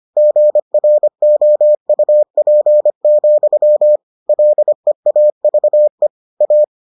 Morse code, heard as a keyed tone, about 25 wpm, 600 Hz. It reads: GROUP, LEAVE A